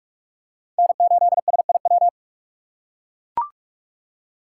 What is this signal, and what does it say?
Morse code, heard as a keyed tone, about 43 wpm, 700 Hz. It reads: N8SIW
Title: USA Call Signs - 5 Characters - 2x Character Spacing - 45wpm